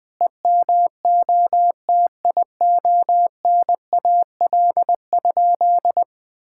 Morse code, heard as a keyed tone, about 20 wpm, 700 Hz. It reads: EMOTIONAL?
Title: Sentences from Top 1000 Words - Review Part 5 - 20wpm